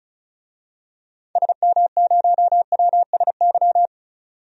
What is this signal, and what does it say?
Morse code, heard as a keyed tone, about 36 wpm, 700 Hz. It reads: SM0WSY
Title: DXCC-100 Call Signs - 6 Characters - 35wpm